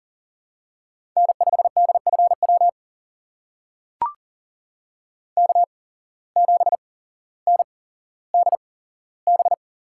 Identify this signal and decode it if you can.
Morse code, heard as a keyed tone, about 39 wpm, spaced as for 17 wpm, 700 Hz. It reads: N5DFW K7NDB
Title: USA Call Signs - 5 Characters - 8x Character Spacing - 40wpm